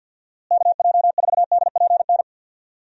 Morse code, heard as a keyed tone, about 52 wpm, 700 Hz. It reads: KJ4DPN
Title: USA Call Signs - 6 Characters - Repeated 3x - 50wpm